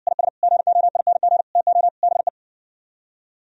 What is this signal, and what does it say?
Morse code, heard as a keyed tone, about 59 wpm, 700 Hz. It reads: IS GOING TO BE